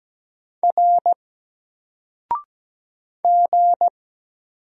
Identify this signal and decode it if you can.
Morse code, heard as a keyed tone, about 17 wpm, 700 Hz. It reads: R G
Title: Single Character (Letter:Number:Punctuation:Prosign) 17wpm